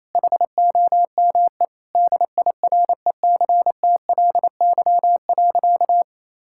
Morse code, heard as a keyed tone, about 28 wpm, 700 Hz. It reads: HOME DIRECTLY.
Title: Sentences from Top 1000 Words - Review Part 5 - 28wpm